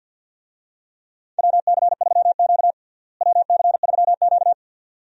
Morse code, heard as a keyed tone, about 52 wpm, 700 Hz. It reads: WC3Q WC3Q
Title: USA Call Signs - 4 Characters - Repeated 2x - 50wpm